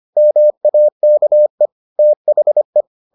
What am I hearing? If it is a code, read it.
Morse code, heard as a keyed tone, about 25 wpm, 600 Hz. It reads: MAKE THE